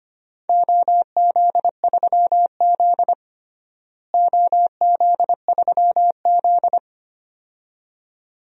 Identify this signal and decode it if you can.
Morse code, heard as a keyed tone, about 25 wpm, 700 Hz. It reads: OZ3Z OZ3Z
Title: DXCC-100 Call Signs - 4 Characters - Repeated 2x - 25wpm